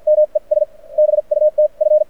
Morse code, heard as a keyed tone, about 52 wpm, spaced as for 43 wpm, 600 Hz. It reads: ME S BUT F
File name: archived / CRSnet